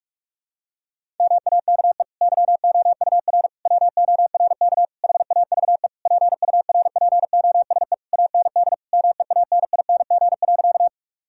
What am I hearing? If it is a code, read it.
Morse code, heard as a keyed tone, about 46 wpm, 700 Hz. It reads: MAKE YOUR WORK HAVE PURPOSE AND MEANING.
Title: Sentences from Top 1000 Words - Review Part 6 - 45wpm